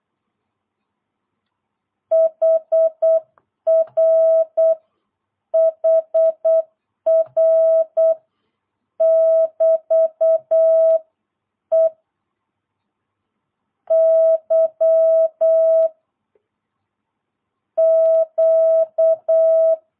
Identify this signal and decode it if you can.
Morse code, heard as a keyed tone, about 8 wpm, 660 Hz. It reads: HR HR <BT> E Y Q
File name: jh8cbh / 2022cw